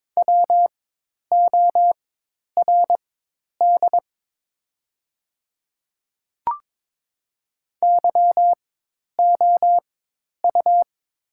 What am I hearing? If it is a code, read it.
Morse code, heard as a keyed tone, about 22 wpm, spaced as for 10 wpm, 700 Hz. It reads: WORD YOU
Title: 4x Character Spacing - Top 100 Words - 22wpm